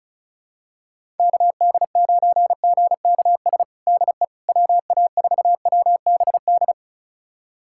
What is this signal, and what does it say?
Morse code, heard as a keyed tone, about 36 wpm, 700 Hz. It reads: KD9GKS DE WA4WBD
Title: USA Callsign DE Callsign - 6 Characters - 35wpm